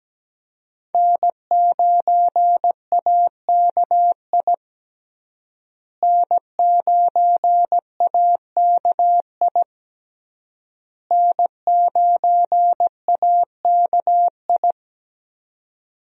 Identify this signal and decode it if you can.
Morse code, heard as a keyed tone, about 17 wpm, 700 Hz. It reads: N9AKI N9AKI N9AKI
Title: USA Call Signs - 5 Characters - Repeated 3x - 17wpm